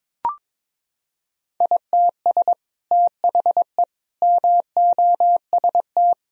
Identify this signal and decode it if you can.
Morse code, heard as a keyed tone, about 22 wpm, 700 Hz. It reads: ITS THE MOST
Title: Sentences from Top 1000 Words - Review Part 4 - 22wpm